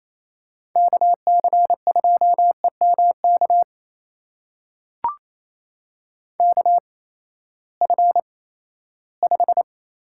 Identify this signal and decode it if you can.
Morse code, heard as a keyed tone, about 28 wpm, spaced as for 16 wpm, 700 Hz. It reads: KC2EMK KF5
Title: USA Call Signs - 6 Characters - 8x Character Spacing - 28wpm